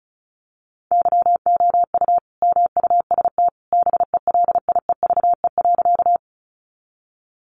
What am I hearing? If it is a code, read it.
Morse code, heard as a keyed tone, about 34 wpm, 700 Hz. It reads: YOU MUST BELIEVE.
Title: Sentences from Top 1000 Words 35wpm